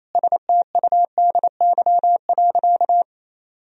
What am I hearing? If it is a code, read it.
Morse code, heard as a keyed tone, about 28 wpm, 700 Hz. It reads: STUDY.
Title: Sentences from Top 1000 Words - Review Part 1 - 28wpm